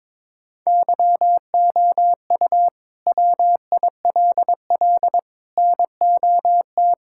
Morse code, heard as a keyed tone, about 22 wpm, 700 Hz. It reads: YOU WILL NOT
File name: practice / 22wpm / Sets of 6 Words - 22wpm